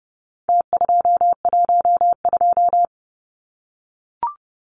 Morse code, heard as a keyed tone, about 30 wpm, 700 Hz. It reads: T212
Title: Numbers - Contest Style - 4-Digit 30wpm